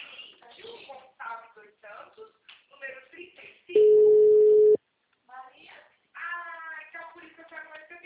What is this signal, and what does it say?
Ringback tone (Europe, 425 Hz), with speech.
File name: out- -2032- - - .8548